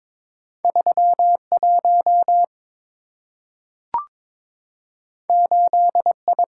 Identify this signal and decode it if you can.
Morse code, heard as a keyed tone, about 22 wpm, 700 Hz. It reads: 31 8I
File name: practice / 22wpm / Two Letter-Number Combinations - 22wpm